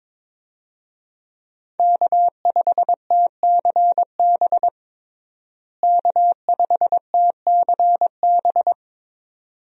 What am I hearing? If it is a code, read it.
Morse code, heard as a keyed tone, about 22 wpm, 700 Hz. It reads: K5TCB K5TCB
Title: USA Call Signs - 5 Characters - Repeated 2x - 22wpm